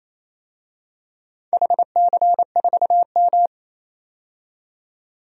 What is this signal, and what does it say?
Morse code, heard as a keyed tone, about 28 wpm, 700 Hz. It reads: HC4M
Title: DXCC-100 Call Signs - 4 Characters - 28wpm